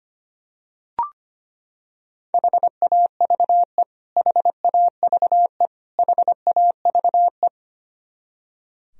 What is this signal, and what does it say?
Morse code, heard as a keyed tone, about 25 wpm, 700 Hz. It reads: HAVE HAVE HAVE
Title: Top 100 Words Repeated 25wpm